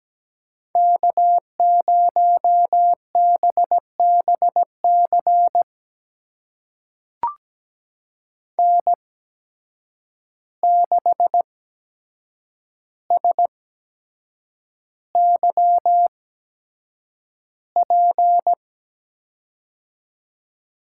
Morse code, heard as a keyed tone, about 17 wpm, spaced as for 7 wpm, 700 Hz. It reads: K0BBC N6SYP
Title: USA Call Signs - 5 Characters - 8x Character Spacing - 17wpm